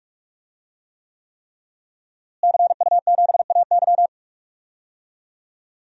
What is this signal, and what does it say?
Morse code, heard as a keyed tone, about 41 wpm, 700 Hz. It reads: CU7AY
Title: DXCC-100 Call Signs - 5 Characters - Repeated 3x - 45wpm